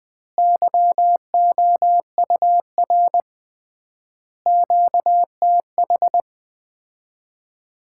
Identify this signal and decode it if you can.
Morse code, heard as a keyed tone, about 21 wpm, 700 Hz. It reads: YOUR QTH
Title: Common QSO Phrases - 3x Word-Spacing - 20wpm